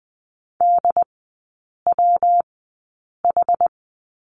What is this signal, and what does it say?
Morse code, heard as a keyed tone, about 20 wpm, spaced as for 8 wpm, 700 Hz. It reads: DWH
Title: 114 Lesson 19e - Letter w - Three Characters 20wpm